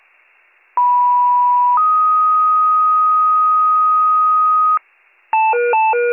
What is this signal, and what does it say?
Two-tone page, heard as a one-second tone, then a three-second tone, about 980 then 1250 Hz. It is heard from 0.8 to 4.8 s.